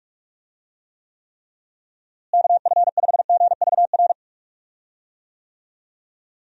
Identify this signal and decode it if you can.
Morse code, heard as a keyed tone, about 46 wpm, 700 Hz. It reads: KF5GVR
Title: USA Call Signs - 6 Characters - Repeated 3x - 45wpm